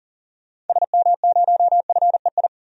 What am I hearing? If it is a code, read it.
Morse code, heard as a keyed tone, about 40 wpm, 700 Hz. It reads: SM0FEI
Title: DXCC-100 Call Signs - 6 Characters - 40wpm